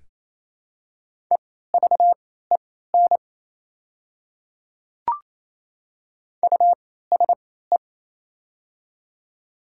Morse code, heard as a keyed tone, about 28 wpm, spaced as for 16 wpm, 700 Hz. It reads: EVEN USE
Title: 3x Character Spacing - Top 500 Words - 28wpm